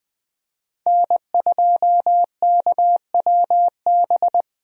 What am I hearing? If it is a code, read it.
Morse code, heard as a keyed tone, about 20 wpm, 700 Hz. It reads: N2KWB
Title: USA Call Signs - 5 Characters - 20wpm